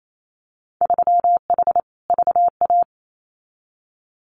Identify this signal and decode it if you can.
Morse code, heard as a keyed tone, about 28 wpm, 700 Hz. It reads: 3H VA
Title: Winter Field Day Exchanges 28wpm